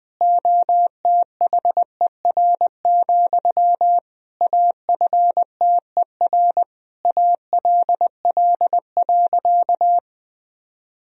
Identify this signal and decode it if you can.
Morse code, heard as a keyed tone, about 20 wpm, 700 Hz. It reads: OTHER, AFTER ALL.
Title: Sentences from Top 1000 Words - Review Part 1 - 20wpm